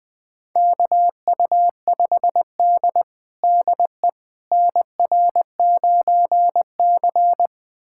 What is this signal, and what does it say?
Morse code, heard as a keyed tone, about 20 wpm, 700 Hz. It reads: KU5D DE NR9C
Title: USA Callsign DE Callsign - 4 Characters - 20wpm